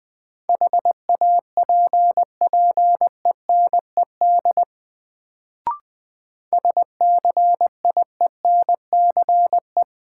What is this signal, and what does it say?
Morse code, heard as a keyed tone, about 20 wpm, 700 Hz. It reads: HAPPENED SCIENCE
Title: Sentences from Top 1000 Words - Review Part 2 - New Word Drill - 20wpm